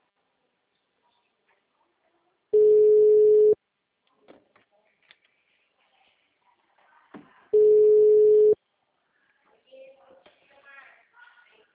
Ringback tone (Europe, 425 Hz).